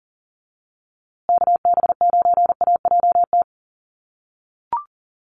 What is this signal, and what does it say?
Morse code, heard as a keyed tone, about 40 wpm, 700 Hz. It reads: KB9AJT
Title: USA Call Signs Repeated 3x 40wpm